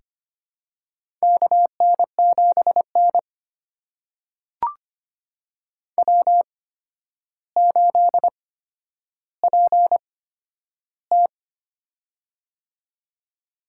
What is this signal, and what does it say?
Morse code, heard as a keyed tone, about 26 wpm, spaced as for 11 wpm, 700 Hz. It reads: KN7N W8PT
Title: USA Call Signs - 4 Characters - 8x Character Spacing - 25wpm